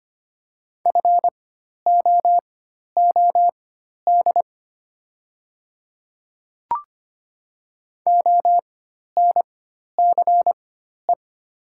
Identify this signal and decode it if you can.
Morse code, heard as a keyed tone, about 25 wpm, spaced as for 12 wpm, 690 Hz. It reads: FOOD ONCE
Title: 4x Character Spacing - Top 300 Words - 25wpm